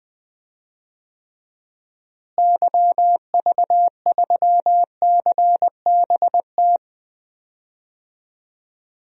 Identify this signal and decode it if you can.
Morse code, heard as a keyed tone, about 20 wpm, 700 Hz. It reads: YV3CBT
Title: DXCC-100 Call Signs - 6 Characters - Repeated 3x - 20wpm